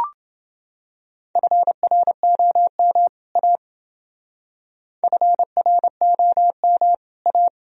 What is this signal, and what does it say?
Morse code, heard as a keyed tone, about 29 wpm, 700 Hz. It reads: FROM A FROM A